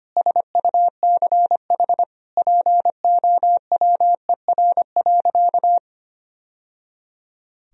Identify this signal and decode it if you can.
Morse code, heard as a keyed tone, about 25 wpm, 700 Hz. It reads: SUCH POWER.
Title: Sentences from Top 1000 Words - Review Part 2 - 25wpm